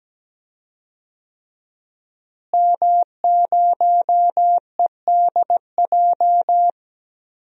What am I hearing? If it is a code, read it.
Morse code, heard as a keyed tone, about 17 wpm, 700 Hz. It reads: M0EDJ
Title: DXCC-100 Call Signs - 5 Characters - Repeated 3x - 17wpm